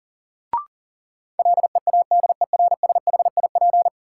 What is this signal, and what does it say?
Morse code, heard as a keyed tone, about 40 wpm, 700 Hz. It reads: LEADERSHIP
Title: Sentences from Top 1600 Words - New Word Drill Alt - 40wpm